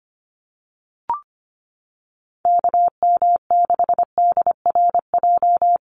Morse code, heard as a keyed tone, about 25 wpm, 700 Hz. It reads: KM6DRJ
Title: USA Call Signs Repeated 3x 25wpm